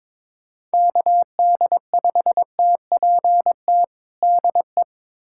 Morse code, heard as a keyed tone, about 21 wpm, 700 Hz. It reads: KD5TPT DE
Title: USA Callsign DE Callsign 22wpm